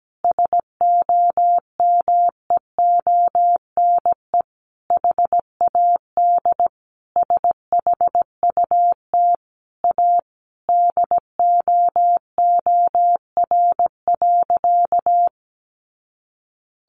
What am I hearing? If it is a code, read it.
Morse code, heard as a keyed tone, about 17 wpm, 700 Hz. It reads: SOMEONE HAD SHUT A DOOR.
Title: A Sound Of Thunder 17wpm